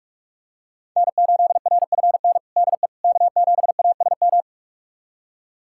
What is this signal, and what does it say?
Morse code, heard as a keyed tone, about 46 wpm, 700 Hz. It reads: N8RFN DE K7ASM
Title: USA Callsign DE Callsign - 5 Characters - 45wpm